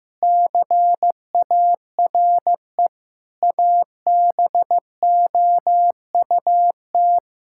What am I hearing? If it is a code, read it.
Morse code, heard as a keyed tone, about 15 wpm, 700 Hz. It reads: CARE ABOUT
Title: Sentences from Top 1000 Words - Review Part 6 - 15wpm